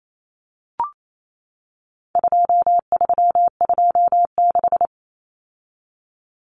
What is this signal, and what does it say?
Morse code, heard as a keyed tone, about 28 wpm, 700 Hz. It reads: 2326
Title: Common Cut Numbers - 4-Digit 28wpm